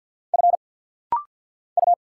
Morse code, heard as a keyed tone, about 50 wpm, 700 Hz. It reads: F U
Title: Single Letters - Rapid-Fire 50wpm